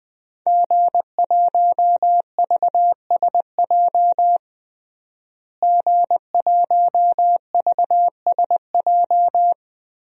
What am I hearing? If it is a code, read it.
Morse code, heard as a keyed tone, about 20 wpm, 700 Hz. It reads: G1VSJ G1VSJ
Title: DXCC-100 Call Signs - 5 Characters - Repeated 2x - 20wpm